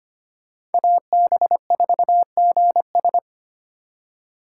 Morse code, heard as a keyed tone, about 27 wpm, 700 Hz. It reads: AB4GS
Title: Speed Racing - USA 2x2 Call Signs 25wpm